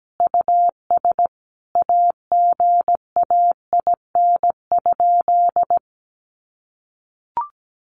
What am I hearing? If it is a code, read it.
Morse code, heard as a keyed tone, about 17 wpm, 700 Hz. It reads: US AGAIN?
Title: Sentences from Top 1200 Words 17wpm